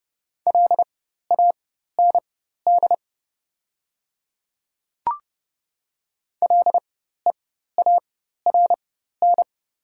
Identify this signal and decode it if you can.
Morse code, heard as a keyed tone, about 30 wpm, spaced as for 14 wpm, 700 Hz. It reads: LAND LEARN
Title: 4x Character Spacing - Top 300 Words - 30wpm